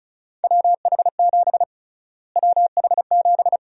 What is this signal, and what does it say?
Morse code, heard as a keyed tone, about 36 wpm, 700 Hz. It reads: WH7 WH7
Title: USA Call Sign Prefixes - Repeated 2x - 35wpm